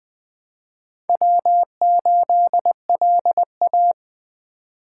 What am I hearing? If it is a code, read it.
Morse code, heard as a keyed tone, about 20 wpm, 700 Hz. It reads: W8LA
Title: Speed Racing - USA 1x2 Call Signs 17wpm